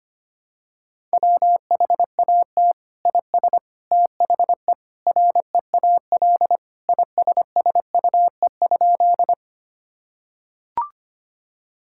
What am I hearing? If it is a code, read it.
Morse code, heard as a keyed tone, about 25 wpm, 700 Hz. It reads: WHAT IS THE REAL ISSUE?
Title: Sentences from Top 1000 Words - Review Part 6 - 25wpm